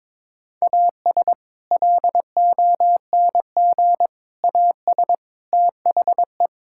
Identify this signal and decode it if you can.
Morse code, heard as a keyed tone, about 21 wpm, 700 Hz. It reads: AS LONG AS THE
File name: practice / 22wpm / Sentences with 4-Word Phrases - 22wpm